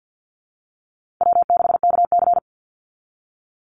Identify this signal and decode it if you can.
Morse code, heard as a keyed tone, about 49 wpm, 700 Hz. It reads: W6KC
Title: Easy Call Signs Repeated 3x - 50wpm